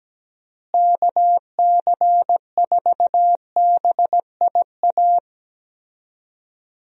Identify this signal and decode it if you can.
Morse code, heard as a keyed tone, about 17 wpm, 700 Hz. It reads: KC4BIA
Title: USA Call Signs - 6 Characters - Repeated 2x - 17wpm